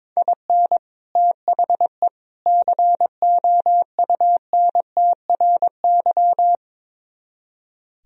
Morse code, heard as a keyed tone, about 22 wpm, 700 Hz. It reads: IN THE COUNTRY
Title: Speed-Racing - Sets of 3 Words 22wpm